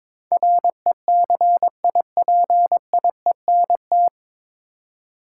Morse code, heard as a keyed tone, about 22 wpm, 700 Hz. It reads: RECIPIENT
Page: Morse Code Ninja practice 22wpm